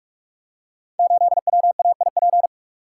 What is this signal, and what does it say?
Morse code, heard as a keyed tone, about 44 wpm, 700 Hz. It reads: 8WAIP
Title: Code Groups - 5 Letter:Numbers - 45wpm